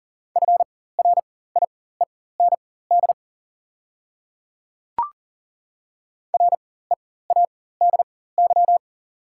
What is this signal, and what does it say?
Morse code, heard as a keyed tone, about 40 wpm, spaced as for 19 wpm, 690 Hz. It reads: FRIEND READY
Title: 4x Character Spacing - Top 500 Words - 40wpm